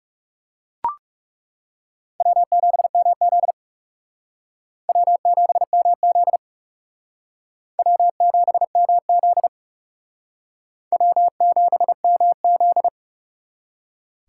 Morse code, heard as a keyed tone, about 37 wpm, 700 Hz. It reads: W7MZ W7MZ W7MZ W7MZ
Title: Speed Racing - USA 1x2 Call Signs 30wpm